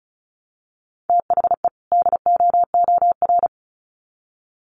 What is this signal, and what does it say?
Morse code, heard as a keyed tone, about 34 wpm, 700 Hz. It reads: THE DOOR
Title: Sets of 2 Words 35wpm